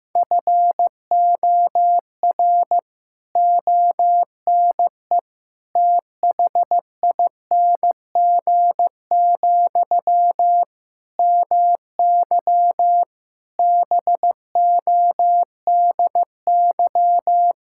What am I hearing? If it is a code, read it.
Morse code, heard as a keyed tone, about 15 wpm, 700 Hz. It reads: FOR ONE THING, MY BODY
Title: Sentences from Top 1000 Words - Review Part 6 - 15wpm